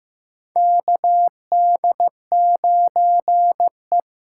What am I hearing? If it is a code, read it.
Morse code, heard as a keyed tone, about 15 wpm, 700 Hz. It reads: KD9E
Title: USA Call Signs - 6 Characters - 8x Character Spacing - 15wpm